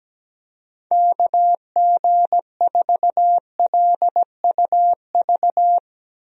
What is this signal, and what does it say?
Morse code, heard as a keyed tone, about 17 wpm, 700 Hz. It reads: KG4LUV
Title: USA Call Signs - 6 Characters - Repeated 3x - 17wpm